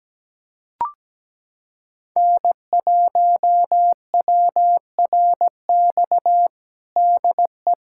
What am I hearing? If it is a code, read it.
Morse code, heard as a keyed tone, about 17 wpm, 700 Hz. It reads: N1WRX DE
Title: USA Callsign DE Callsign - 5 Characters - 17wpm